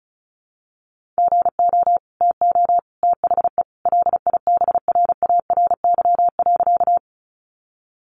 Morse code, heard as a keyed tone, about 34 wpm, 700 Hz. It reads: GO TO THE LIBRARY.
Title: Sentences from Top 600 Words 35wpm